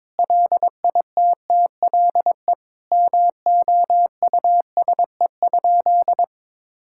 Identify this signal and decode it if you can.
Morse code, heard as a keyed tone, about 22 wpm, 700 Hz. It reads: LITTLE MOUSE?
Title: Sentences from Top 1000 Words - Review Part 5 - 22wpm